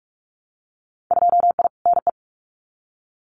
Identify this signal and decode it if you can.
Morse code, heard as a keyed tone, about 46 wpm, 700 Hz. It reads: 2I NE
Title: Winter Field Day Exchanges Encore 45wpm